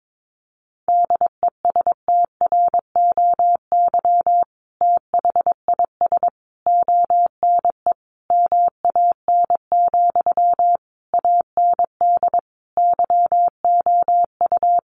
Morse code, heard as a keyed tone, about 22 wpm, 700 Hz. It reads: DESTROY THIS ONE MAN, AND YOU
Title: A Sound Of Thunder 22wpm